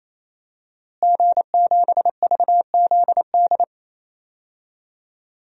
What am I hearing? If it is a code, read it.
Morse code, heard as a keyed tone, about 28 wpm, 700 Hz. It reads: G7VZD
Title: DXCC-100 Call Signs - 5 Characters - Repeated 3x - 28wpm